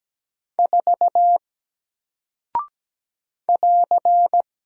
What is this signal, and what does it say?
Morse code, heard as a keyed tone, about 17 wpm, 700 Hz. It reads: 4 <AR>
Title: Single Character (Letter:Number:Punctuation:Prosign) Alt 17wpm